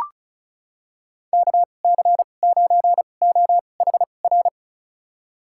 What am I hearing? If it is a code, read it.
Morse code, heard as a keyed tone, about 36 wpm, spaced as for 25 wpm, 700 Hz. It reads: KC9OHR
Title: USA Call Signs - 6 Characters - 2x Character Spacing - 35wpm